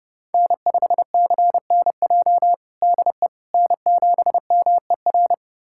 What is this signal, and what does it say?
Morse code, heard as a keyed tone, about 32 wpm, 700 Hz. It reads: N5CNJ DE N7MER
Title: USA Callsign DE Callsign - 5 Characters - 30wpm